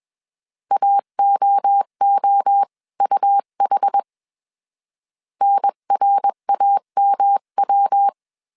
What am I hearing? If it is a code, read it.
Morse code, heard as a keyed tone, about 21 wpm, 800 Hz. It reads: AOOUH NRAMW
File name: CW / DET / 1 3 1500C S B 20wpm 1754